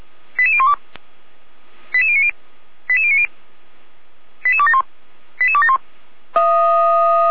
Five-tone selective call (ZVEI), heard as quick runs of five tones, tones of about 80 ms. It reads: 88012 88090 88090 88271 88271
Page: ZVEI